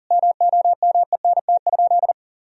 Morse code, heard as a keyed tone, about 40 wpm, 700 Hz. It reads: MOMENT?
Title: Sentences from Top 1000 Words - Review Part 1 - 40wpm